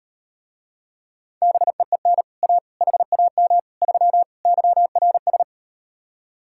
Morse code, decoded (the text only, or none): BEEN A HAM 3 YRS